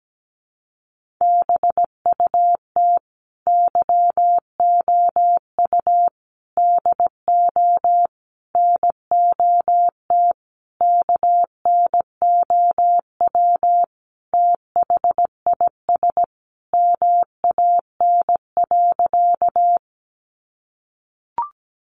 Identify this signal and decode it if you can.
Morse code, heard as a keyed tone, about 17 wpm, 700 Hz. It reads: BUT YOU DO NOT KNOW THIS MAN.
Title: Sentences from Top 600 Words 17wpm